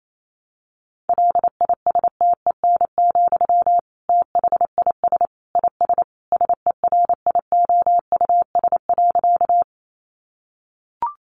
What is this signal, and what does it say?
Morse code, heard as a keyed tone, about 28 wpm, 700 Hz. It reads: LISTEN, THIS IS SERIOUS.
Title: Sentences from Top 1100 Words 28wpm